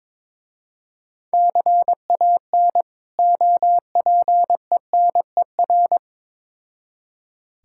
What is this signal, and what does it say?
Morse code, heard as a keyed tone, about 23 wpm, 700 Hz. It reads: CAN OPENER
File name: Difficult Words with Hints - 22wpm